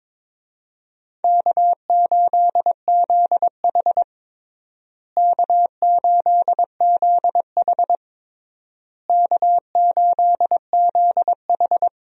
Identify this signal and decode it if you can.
Morse code, heard as a keyed tone, about 22 wpm, 700 Hz. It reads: K8ZH K8ZH K8ZH
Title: USA Call Signs - 4 Characters - Repeated 3x - 22wpm